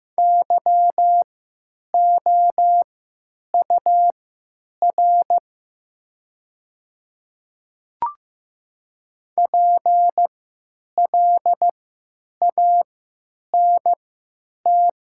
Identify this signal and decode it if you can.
Morse code, heard as a keyed tone, about 16 wpm, spaced as for 9 wpm, 700 Hz. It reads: YOUR PLANT
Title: 3x Character Spacing - Top 500 Words - 15wpm